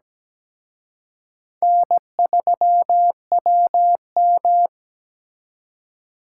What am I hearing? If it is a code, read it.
Morse code, heard as a keyed tone, about 20 wpm, 700 Hz. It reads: N3WM
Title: Speed Racing - USA 1x2 Call Signs 17wpm